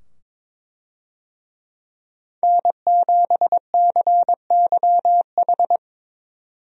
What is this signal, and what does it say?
Morse code, heard as a keyed tone, about 22 wpm, 700 Hz. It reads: N7CYH
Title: USA Call Signs - 5 Characters - 22wpm